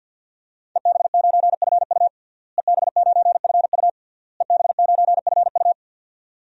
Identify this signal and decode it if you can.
Morse code, heard as a keyed tone, about 52 wpm, 700 Hz. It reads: EB9FU EB9FU EB9FU
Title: DXCC-100 Call Signs - 5 Characters - Repeated 3x - 50wpm